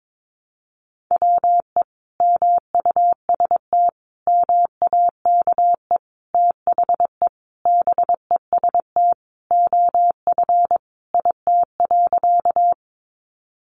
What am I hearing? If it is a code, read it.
Morse code, heard as a keyed tone, about 22 wpm, 700 Hz. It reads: WE MUST MAKE THE BEST OF IT.
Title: Sentences from Top 400 Words 22wpm